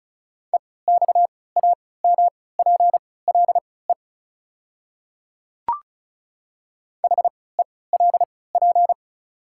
Morse code, heard as a keyed tone, about 34 wpm, spaced as for 20 wpm, 700 Hz. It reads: EXAMPLE HELP
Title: 3x Character Spacing - Top 300 Words - 35wpm